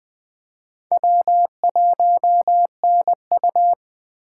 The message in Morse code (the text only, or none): W1NU